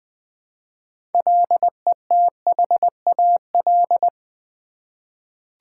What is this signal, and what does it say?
Morse code, heard as a keyed tone, about 20 wpm, 700 Hz. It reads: LETHAL
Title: 6 Letter Words - First 3 Letters Given - 20wpm